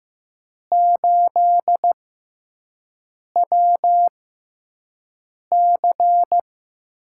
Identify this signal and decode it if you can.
Morse code, heard as a keyed tone, about 15 wpm, spaced as for 5 wpm, 700 Hz. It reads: 8WC
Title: USA Call Signs - 5 Characters - 6x Character Spacing - 15wpm